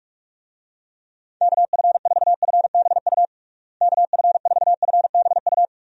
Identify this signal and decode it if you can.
Morse code, heard as a keyed tone, about 42 wpm, 700 Hz. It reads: KF4FBU KF4FBU
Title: USA Call Signs - 6 Characters - Repeated 2x - 45wpm